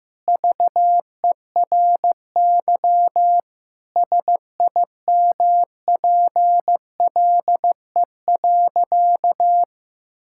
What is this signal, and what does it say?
Morse code, heard as a keyed tone, about 15 wpm, 700 Hz. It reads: VERY SIMPLE.
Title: Sentences from Top 1000 Words - Review Part 6 - 15wpm